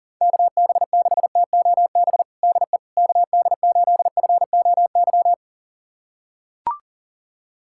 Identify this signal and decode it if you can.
Morse code, heard as a keyed tone, about 40 wpm, 700 Hz. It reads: KB6TOB DE KD8FOY
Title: USA Callsign DE Callsign 40wpm